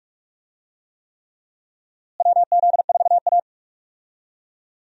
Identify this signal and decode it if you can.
Morse code, heard as a keyed tone, about 43 wpm, 700 Hz. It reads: WZ4A
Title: USA Call Signs - 4 Characters - Repeated 3x - 45wpm